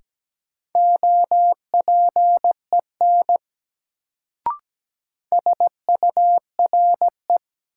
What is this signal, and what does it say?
Morse code, heard as a keyed tone, about 17 wpm, 700 Hz. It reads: OPEN SURE
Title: Sentences from Top 300 Words - New Word Drill - 17wpm